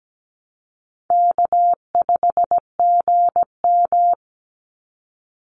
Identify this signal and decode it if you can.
Morse code, heard as a keyed tone, about 17 wpm, 700 Hz. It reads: K5GM
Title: Easy Call Signs Repeated 2x - 17wpm